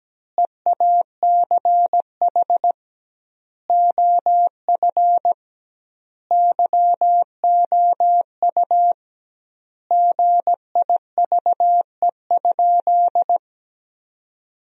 Morse code, heard as a keyed tone, about 17 wpm, 700 Hz. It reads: EACH OF YOU GIVE?
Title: 2x Word-Spacing - Sentences from Top 200 Words - 17wpm